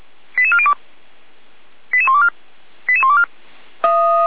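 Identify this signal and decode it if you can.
Five-tone selective call (ZVEI), heard as quick runs of five tones, tones of about 70 ms. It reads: 88402 88124 88124
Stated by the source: Five-tone ZVEI selective call received on an SDR.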